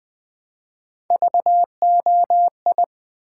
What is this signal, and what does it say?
Morse code, heard as a keyed tone, about 20 wpm, 700 Hz. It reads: VOI
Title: Build A Word - 1 Letter At A Time - 5-Letter Words - 20wpm